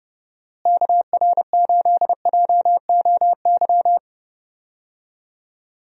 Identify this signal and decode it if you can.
Morse code, heard as a keyed tone, about 32 wpm, 700 Hz. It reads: KR8JOY
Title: USA Call Signs - 6 Characters - Repeated 3x - 30wpm